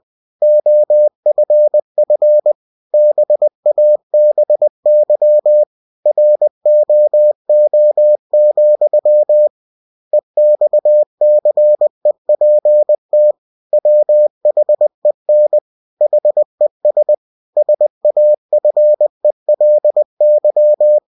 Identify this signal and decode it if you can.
Morse code, heard as a keyed tone, about 20 wpm, 600 Hz. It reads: OFF BABY ROO, EXCEPT WHEN HES SAFELY
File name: cw-books / winnie-the-pooh / winnie-20wpm-page-0063